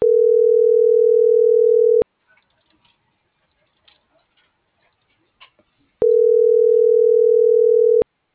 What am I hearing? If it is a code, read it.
Ringback tone (North America), 440 + 480 Hz.